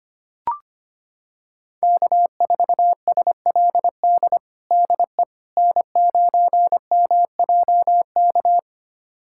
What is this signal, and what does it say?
Morse code, heard as a keyed tone, about 25 wpm, 700 Hz. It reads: K4SLD DE N9MJK
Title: USA Callsign DE Callsign - 5 Characters - 25wpm